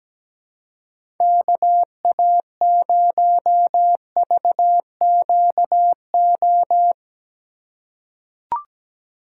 Morse code, heard as a keyed tone, about 17 wpm, 700 Hz. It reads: KA0VQO
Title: USA Call Signs - 6 Characters - 2x Character Spacing - 17wpm